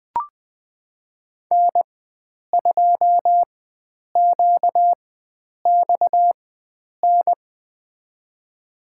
Morse code, heard as a keyed tone, about 20 wpm, spaced as for 9 wpm, 700 Hz. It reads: N2QXN
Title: USA Call Signs - 5 Characters - 4x Character Spacing - 20wpm